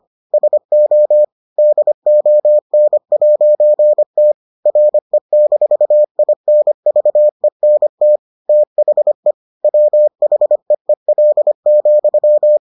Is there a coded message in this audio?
Yes: Morse code from 0.3 to 12.6 s, heard as a keyed tone, about 25 wpm, 600 Hz. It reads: SO DON'T RE-INVENT THE WHEEL,